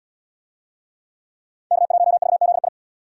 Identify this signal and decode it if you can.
Morse code, heard as a keyed tone, about 67 wpm, 700 Hz. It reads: K1VZI